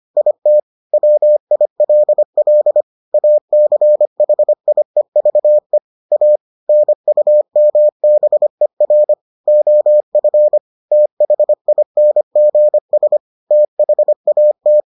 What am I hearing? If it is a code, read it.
Morse code, heard as a keyed tone, about 25 wpm, 600 Hz. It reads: IT WILL ACHIEVE A NUMBER OF THINGS THAT